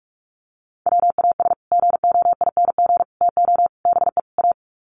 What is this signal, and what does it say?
Morse code, heard as a keyed tone, about 44 wpm, 700 Hz. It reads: WAS GOING TO BE A
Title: Sets of 5 Words 45wpm